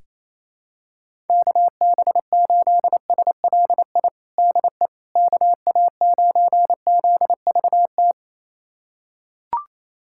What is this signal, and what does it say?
Morse code, heard as a keyed tone, about 28 wpm, 700 Hz. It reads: KB8SLI DE KA9ZVT
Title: USA Callsign DE Callsign - 6 Characters - 28wpm